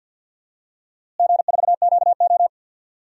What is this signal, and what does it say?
Morse code, heard as a keyed tone, about 52 wpm, 700 Hz. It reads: G4QO
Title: DXCC-100 Call Signs - 4 Characters - Repeated 3x - 50wpm